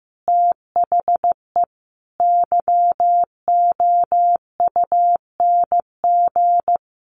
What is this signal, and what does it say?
Morse code, heard as a keyed tone, about 15 wpm, 700 Hz. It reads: THE YOUNG
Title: Sentences from Top 900 Words 15wpm